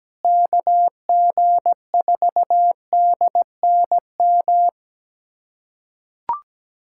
Morse code, heard as a keyed tone, about 17 wpm, 700 Hz. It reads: KG4DNM
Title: USA Callsign DE Callsign 17wpm